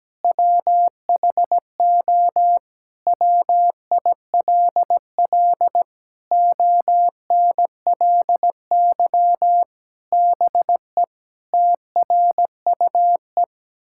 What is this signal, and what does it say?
Morse code, heard as a keyed tone, about 17 wpm, 700 Hz. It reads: WHO WILL ONLY BE TRUE
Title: Sentences from Top 1000 Words - Review Part 6 - 17wpm